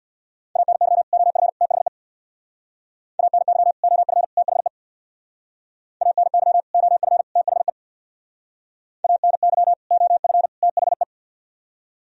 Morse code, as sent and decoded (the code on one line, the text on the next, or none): .- -. -.-- / --- ..-. / - .... / .- -. -.-- / --- ..-. / - .... / .- -. -.-- / --- ..-. / - .... / .- -. -.-- / --- ..-. / - .... .
ANY OF THE ANY OF THE ANY OF THE ANY OF THE